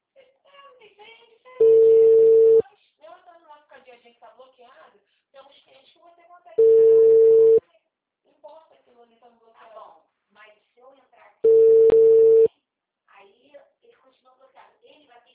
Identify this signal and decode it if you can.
Ringback tone (China, 450 Hz), with speech.